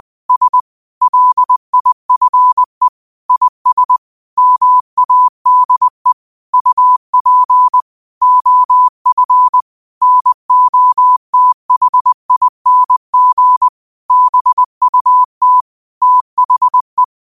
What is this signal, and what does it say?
Morse code, heard as a keyed tone, about 20 wpm, 1000 Hz. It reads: S LIFE IS MADE UP OF NOTHING BUT THE